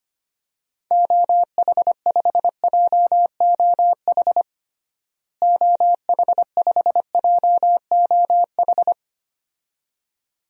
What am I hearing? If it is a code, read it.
Morse code, heard as a keyed tone, about 25 wpm, 700 Hz. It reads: OH5JOH OH5JOH
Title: DXCC-100 Call Signs - 6 Characters - Repeated 2x - 25wpm